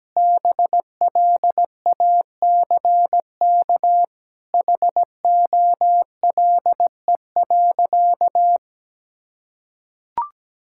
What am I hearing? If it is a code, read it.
Morse code, heard as a keyed tone, about 17 wpm, 700 Hz. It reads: BLACK HOLE.
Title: Sentences from Top 1600 Words 17wpm